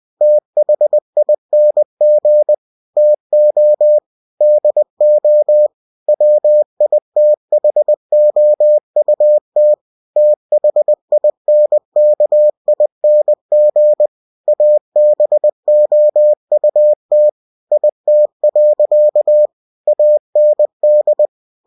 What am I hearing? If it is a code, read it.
Morse code, heard as a keyed tone, about 20 wpm, 600 Hz. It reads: THING TO DO WITHOUT THINKING ABOUT IT. AND
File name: cw-books / winnie-the-pooh / winnie-20wpm-page-0092